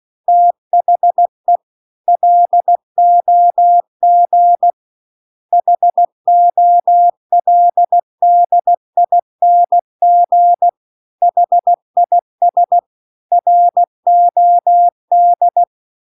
Morse code, heard as a keyed tone, about 16 wpm, 700 Hz. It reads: THE LOG HOLDING HIS ROD
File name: CW-The-Big-Twoheaded-River-2-16WPM